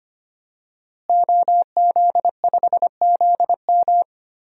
Morse code, heard as a keyed tone, about 25 wpm, 700 Hz. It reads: OZ5ZM
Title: DXCC-100 Call Signs - 5 Characters - Repeated 3x - 25wpm